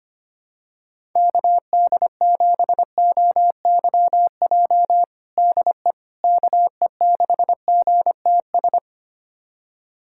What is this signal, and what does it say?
Morse code, heard as a keyed tone, about 25 wpm, 700 Hz. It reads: KD7OYJ DE KE6GTS
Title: USA Callsign DE Callsign - 6 Characters - 25wpm